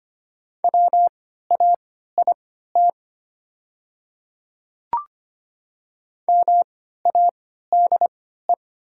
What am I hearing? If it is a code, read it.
Morse code, heard as a keyed tone, about 25 wpm, spaced as for 14 wpm, 700 Hz. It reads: WAIT MADE
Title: 3x Character Spacing - Top 500 Words - 25wpm